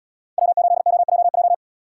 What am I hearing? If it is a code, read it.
Morse code, heard as a keyed tone, about 77 wpm, 700 Hz. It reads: W8OJJ